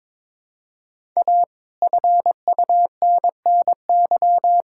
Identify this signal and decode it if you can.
Morse code, heard as a keyed tone, about 22 wpm, 700 Hz. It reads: A FUNNY
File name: Sets of 9 Words - 22wpm